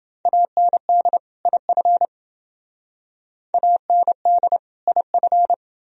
Morse code, heard as a keyed tone, about 28 wpm, 700 Hz. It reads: AND IF AND IF